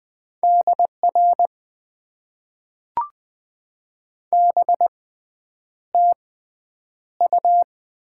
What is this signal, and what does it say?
Morse code, heard as a keyed tone, about 20 wpm, spaced as for 9 wpm, 700 Hz. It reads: DR BTU
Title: Top 90 QSO Elements - 6x Character Spacing - 20wpm